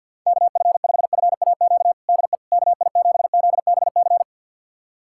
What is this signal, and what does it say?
Morse code, heard as a keyed tone, about 50 wpm, 700 Hz. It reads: KF5FAQ DE KI7ZBC
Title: USA Callsign DE Callsign 50wpm